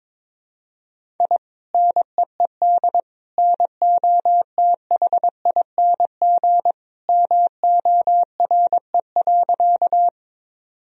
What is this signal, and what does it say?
Morse code, heard as a keyed tone, about 22 wpm, 700 Hz. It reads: I NEED NOTHING MORE.
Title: Sentences from Top 1000 Words - Review Part 1 - 22wpm